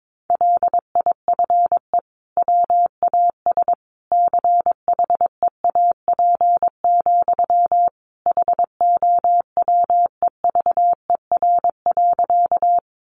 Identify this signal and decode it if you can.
Morse code, heard as a keyed tone, about 22 wpm, 700 Hz. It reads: LIFE WAS CHEAP, HOWEVER.
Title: Sentences from Top 1200 Words 22wpm